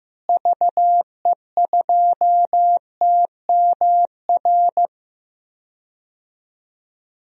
Morse code, heard as a keyed tone, about 15 wpm, 700 Hz. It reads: VE2TMR
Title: DXCC-100 Call Signs - 6 Characters - Repeated 3x - 15wpm